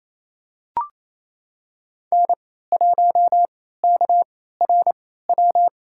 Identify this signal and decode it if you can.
Morse code, heard as a keyed tone, about 27 wpm, spaced as for 16 wpm, 700 Hz. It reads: N1KRW
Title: USA Call Signs - 3x Character Spacing - 28wpm